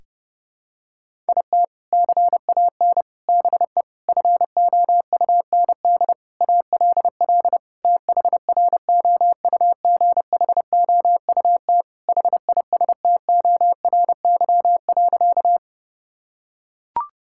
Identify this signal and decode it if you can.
Morse code, heard as a keyed tone, about 30 wpm, 700 Hz. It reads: IT CAN BE FOUND ALL THROUGHOUT HISTORY.
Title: Sentences from Top 1000 Words - Review Part 5 - 30wpm